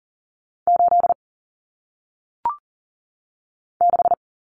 Morse code, heard as a keyed tone, about 39 wpm, 700 Hz. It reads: 8 6
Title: Numbers - 1-Digit 40wpm